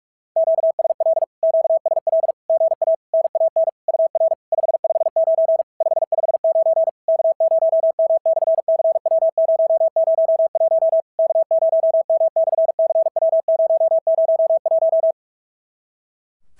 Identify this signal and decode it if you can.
Morse code, heard as a keyed tone, about 46 wpm, 650 Hz. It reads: QSL QSL GA NAN UR 559 559 K0M/CW001 K0M/CW001
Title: SOTA QSO Practice 45wpm